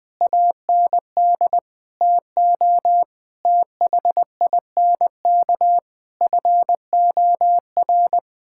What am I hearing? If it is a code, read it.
Morse code, heard as a keyed tone, about 20 wpm, 700 Hz. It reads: AND TO THINK FOR
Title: Sentences from Top 1000 Words - Review Part 6 - 20wpm